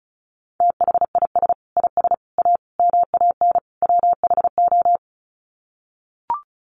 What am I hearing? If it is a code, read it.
Morse code, heard as a keyed tone, about 34 wpm, 700 Hz. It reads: THIS IS A MAN WHO
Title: Sets of 5 Words Encore2 35wpm